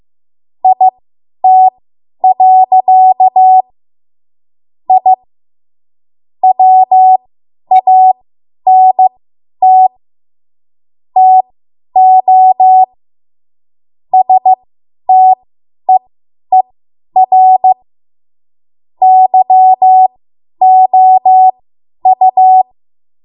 Morse code, heard as a keyed tone, about 15 wpm, spaced as for 10 wpm, 750 Hz. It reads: IT. I WANT TO STEER YOU
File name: Morse / 10WPM